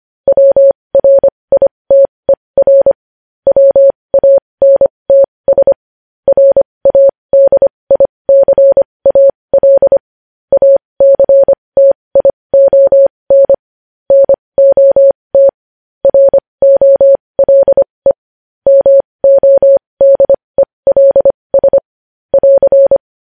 Morse code, heard as a keyed tone, about 25 wpm, spaced as for 20 wpm, 550 Hz. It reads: WRITER WANTS RADICAL ACTION NOT ROLE MODELS <AR>
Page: news-headlines-in-morse-code-at-20-wpm